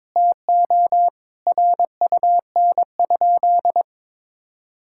Morse code, heard as a keyed tone, about 22 wpm, 700 Hz. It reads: TO RUN?
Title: Sentences from Top 1000 Words - Review Part 6 - 22wpm